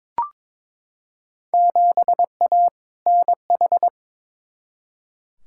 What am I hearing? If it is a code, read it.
Morse code, heard as a keyed tone, about 22 wpm, 700 Hz. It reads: 7A NH
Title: Field Day Exchanges - 22wpm